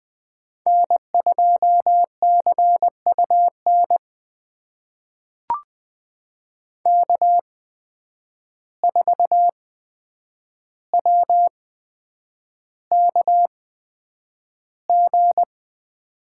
Morse code, heard as a keyed tone, about 20 wpm, spaced as for 9 wpm, 700 Hz. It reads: N2CUN K4WKG
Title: USA Call Signs - 5 Characters - 8x Character Spacing - 20wpm